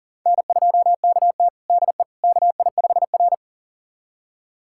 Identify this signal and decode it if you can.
Morse code, heard as a keyed tone, about 40 wpm, 700 Hz. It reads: N2KT DE KI5R
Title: USA Callsign DE Callsign - 4 Characters - 40wpm